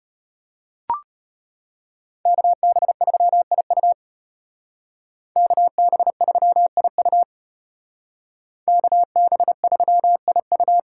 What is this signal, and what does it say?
Morse code, heard as a keyed tone, about 33 wpm, 700 Hz. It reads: KB3IU KB3IU KB3IU